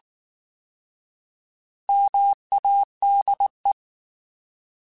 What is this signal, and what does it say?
Morse code, heard as a keyed tone, about 19 wpm, 800 Hz. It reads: MADE
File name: Alphabetical / Alphabetical mp3s / Ms @18